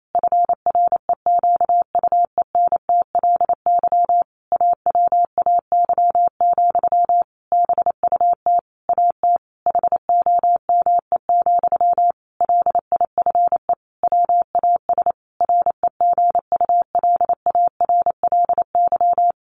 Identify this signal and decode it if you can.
Morse code, heard as a keyed tone, about 28 wpm, 700 Hz. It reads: FREQUENTLY AWAY, BUT AT HOME, LIFE WAS REGULARLY
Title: Sentences from Top 1400 Words 28wpm